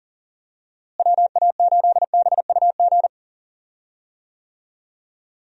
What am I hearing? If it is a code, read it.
Morse code, heard as a keyed tone, about 40 wpm, 700 Hz. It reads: WA8BUG
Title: USA Call Signs - 6 Characters - Repeated 2x - 40wpm